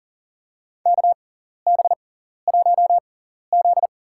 Morse code, heard as a keyed tone, about 40 wpm, spaced as for 14 wpm, 700 Hz. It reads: KB1Z
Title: USA Call Signs - 4 Characters - 6x Character Spacing - 40wpm